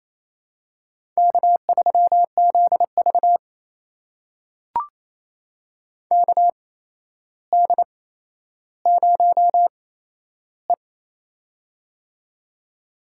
Morse code, heard as a keyed tone, about 28 wpm, spaced as for 12 wpm, 700 Hz. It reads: K3ZV KD0E
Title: USA Call Signs - 4 Characters - 8x Character Spacing - 28wpm